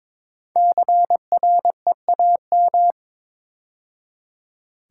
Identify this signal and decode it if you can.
Morse code, heard as a keyed tone, about 22 wpm, 690 Hz. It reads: CREAM
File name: 5 Letter Words - First 3 Letters Given - 22wpm